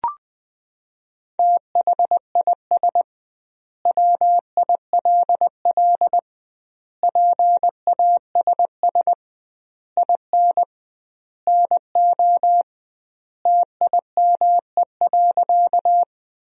Morse code, heard as a keyed tone, about 20 wpm, 700 Hz. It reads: THIS WILL PASS IN NO TIME.
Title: 2x Word-Spacing - Sentences from Top 400 Words - 20wpm